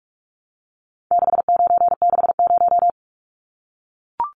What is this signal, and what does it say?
Morse code, heard as a keyed tone, about 44 wpm, 700 Hz. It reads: 6960
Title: Numbers - 4-Digit 45wpm